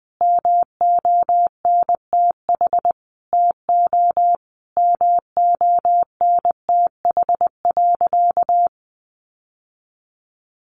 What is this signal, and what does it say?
Morse code, heard as a keyed tone, about 20 wpm, 700 Hz. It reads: MONTH TO MONTH.
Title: Sentences from Top 1200 Words 20wpm